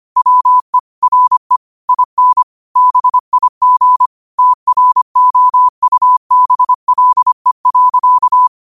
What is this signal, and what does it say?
Morse code, heard as a keyed tone, about 25 wpm, 1000 Hz. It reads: WE RE IN BIG TROUBLE.